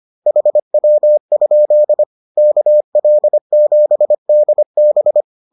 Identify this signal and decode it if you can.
Morse code, heard as a keyed tone, about 25 wpm, 600 Hz. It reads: HW? KL7DB